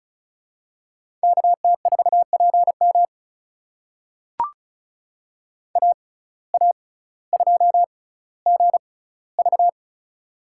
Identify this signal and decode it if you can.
Morse code, heard as a keyed tone, about 36 wpm, spaced as for 18 wpm, 700 Hz. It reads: KT4PM AA2GV
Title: USA Call Signs - 5 Characters - 6x Character Spacing - 35wpm